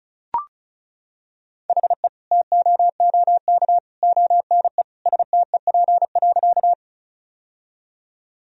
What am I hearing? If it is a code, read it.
Morse code, heard as a keyed tone, about 34 wpm, 700 Hz. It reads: HE TOOK ONE STEP.
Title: Sentences from Top 1000 Words - Review Part 2 - 35wpm